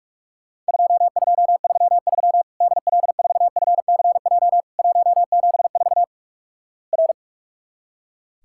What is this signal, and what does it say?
Morse code, heard as a keyed tone, about 46 wpm, 700 Hz. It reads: 2233 DL4FCJ 174 R
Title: Worked All Europe DX Contest - QTC Exchange - 45wpm